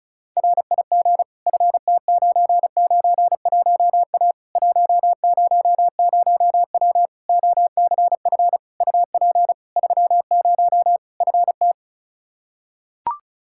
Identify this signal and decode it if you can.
Morse code, heard as a keyed tone, about 34 wpm, 700 Hz. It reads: RIG FT991A 100W OCF UP 30 FT
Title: Practice QSO - Exchange 2 - Info - 35wpm